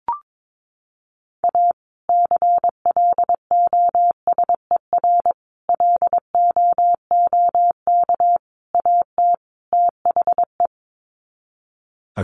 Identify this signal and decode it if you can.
Morse code, heard as a keyed tone, about 22 wpm, 700 Hz. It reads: A CLOSER LOOK AT THE
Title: Sets of 5 Words Encore 22wpm